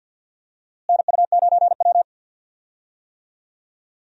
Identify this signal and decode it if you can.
Morse code, heard as a keyed tone, about 50 wpm, 700 Hz. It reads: NU9W
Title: USA Call Signs - 4 Characters - Repeated 3x - 50wpm